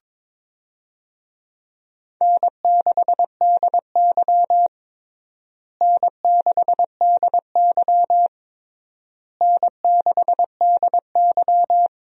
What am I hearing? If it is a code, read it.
Morse code, heard as a keyed tone, about 22 wpm, 700 Hz. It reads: N6DY N6DY N6DY
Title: USA Call Signs - 4 Characters - Repeated 3x - 22wpm